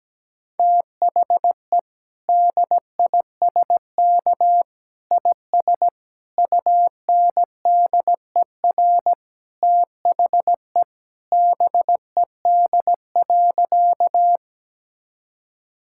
Morse code, heard as a keyed tone, about 17 wpm, 700 Hz. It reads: THE DISK IS UNDER THE BED.
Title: Sentences from Top 1000 Words - Review Part 6 - 17wpm